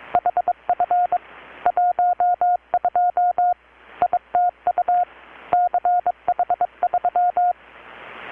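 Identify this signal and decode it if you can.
Morse code, heard as a keyed tone, about 20 wpm, 720 Hz. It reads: HF 12 ITU CH3
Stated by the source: Morse from KPH's Night of Nights XX maritime radiotelegraphy event, received in Chiapas, Mexico about 3,800 km away on a Kenwood R-600 with a grounded 10 m long-wire antenna.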